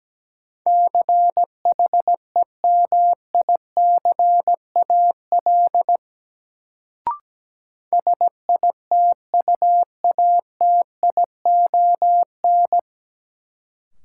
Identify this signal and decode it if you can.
Morse code, heard as a keyed tone, about 17 wpm, 700 Hz. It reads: CHEMICAL SITUATION
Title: Sentences from Top 1200 Words - New Word Drill - 17wpm